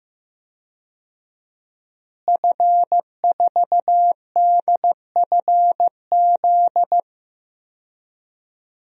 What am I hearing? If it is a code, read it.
Morse code, heard as a keyed tone, about 15 wpm, 700 Hz. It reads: F4DFZ
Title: DXCC-100 Call Signs - 5 Characters - Repeated 2x - 15wpm